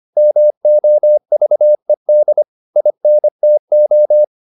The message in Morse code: MOVED INTO

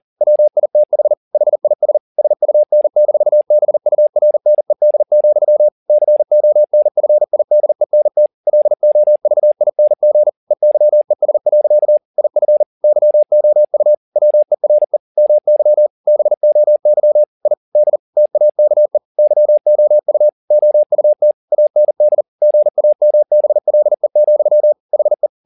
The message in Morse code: WITH HIS SUN-BURNED, CONFIDENT LOVING EYES. IF YOU WERE MY BOY I D TAKE YOU OUT AND GAMBLE, HE

